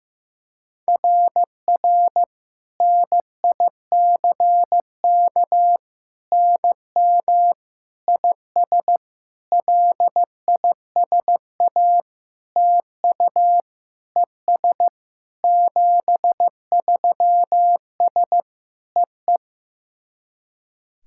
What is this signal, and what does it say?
Morse code, heard as a keyed tone, about 15 wpm, 700 Hz. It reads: RR NICK NM IS LISA TU ES 73S EE
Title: SOTA QSO Practice 15wpm